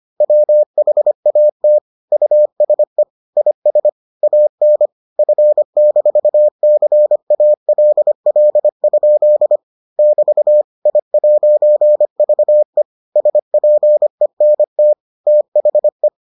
Morse code, heard as a keyed tone, about 25 wpm, 600 Hz. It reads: WHAT USE IS AN F-CALL? <BT> I'VE SPENT THE